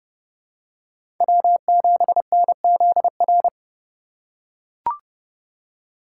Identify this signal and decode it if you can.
Morse code, heard as a keyed tone, about 30 wpm, 700 Hz. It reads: W7NZR
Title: USA Call Signs - 5 Characters - 2x Character Spacing - 30wpm